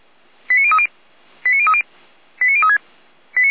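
Five-tone selective call (ZVEI), heard as quick runs of five tones, tones of about 70 ms. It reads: 89030 89030 89036
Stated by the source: ZVEI five-tone selective call 89062 received on an SDR.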